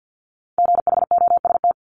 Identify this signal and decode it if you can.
Morse code, heard as a keyed tone, about 50 wpm, 700 Hz. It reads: GHOST
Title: Multi-letter Phonemes 50wpm